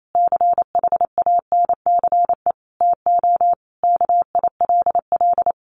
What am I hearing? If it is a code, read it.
Morse code, heard as a keyed tone, about 28 wpm, 700 Hz. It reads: CHANCE TO KILL
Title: Sentences from Top 1400 Words 28wpm